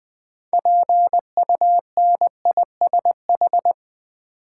Morse code, heard as a keyed tone, about 20 wpm, 700 Hz. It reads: PUNISH
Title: Build A Word - 2 Letters At A Time - 6-Letter Words - 20wpm